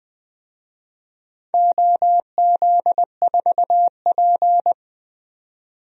Morse code, heard as a keyed tone, about 21 wpm, 700 Hz. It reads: OZ4P
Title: DXCC-100 Call Signs - 4 Characters - 20wpm